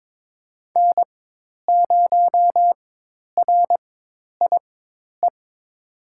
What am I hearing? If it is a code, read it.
Morse code, heard as a keyed tone, about 22 wpm, spaced as for 10 wpm, 700 Hz. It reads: N0RIE
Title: USA Call Signs - 5 Characters - 4x Character Spacing - 22wpm